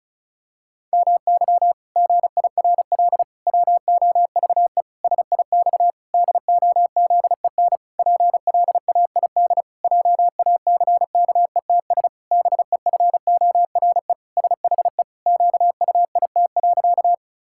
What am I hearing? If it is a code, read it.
Morse code, heard as a keyed tone, about 36 wpm, 700 Hz. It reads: MY GIRL WOVE SIX DOZEN PLAID JACKETS BEFORE SHE QUIT.
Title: Pangram Challenge v2 - 35wpm